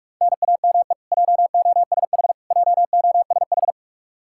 Morse code, heard as a keyed tone, about 46 wpm, 700 Hz. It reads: NAME JOSH JOSH
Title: K1USN SST QSO Practice 45wpm